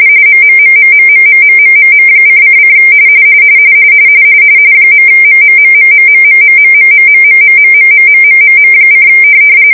RTTY, about 45.45 baud. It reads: MMMMMMMMMMM::: .MMMMMMMMMMMMMMMMMMMMM: